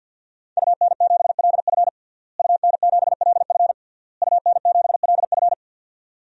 Morse code, heard as a keyed tone, about 52 wpm, 700 Hz. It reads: UN7LF UN7LF UN7LF
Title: DXCC-100 Call Signs - 5 Characters - Repeated 3x - 50wpm